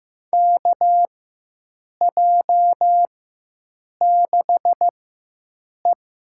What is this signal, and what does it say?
Morse code, heard as a keyed tone, about 16 wpm, spaced as for 7 wpm, 700 Hz. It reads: KJ6E
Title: USA Call Signs - 4x Character Spacing - 15wpm